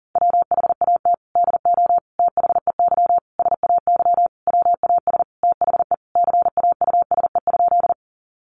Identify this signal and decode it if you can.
Morse code, heard as a keyed tone, about 40 wpm, 700 Hz. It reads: WHAT DO THEY SAY WAS THE CAUSE?
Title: Sentences from Top 200 Words 40wpm